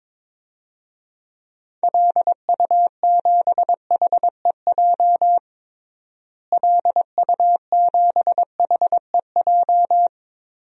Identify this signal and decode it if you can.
Morse code, heard as a keyed tone, about 22 wpm, 700 Hz. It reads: LU7HEJ LU7HEJ
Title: DXCC-100 Call Signs - 6 Characters - Repeated 3x - 22wpm